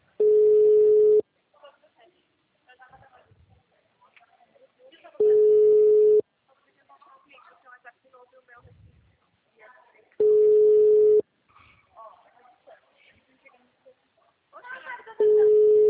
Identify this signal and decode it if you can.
Ringback tone (Europe, 425 Hz), with speech.